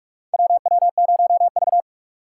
Morse code, heard as a keyed tone, about 46 wpm, 700 Hz. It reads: WW0V
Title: USA Call Signs - 4 Characters - 3x Character Spacing - 45wpm